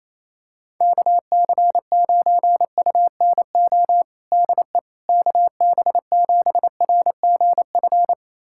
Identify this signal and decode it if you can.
Morse code, heard as a keyed tone, about 28 wpm, 700 Hz. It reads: KC9UNO DE KB7RGF
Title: USA Callsign DE Callsign 28wpm